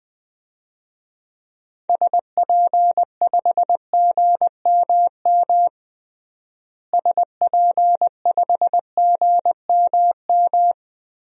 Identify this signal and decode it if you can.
Morse code, heard as a keyed tone, about 20 wpm, 700 Hz. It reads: SP5GMM SP5GMM
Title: DXCC-100 Call Signs - 6 Characters - Repeated 2x - 20wpm